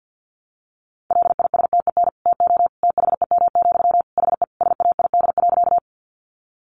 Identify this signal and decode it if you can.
Morse code, heard as a keyed tone, about 50 wpm, 700 Hz. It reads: LISTEN TO THEM, HE SAID.
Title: Sentences from Top 400 Words 50wpm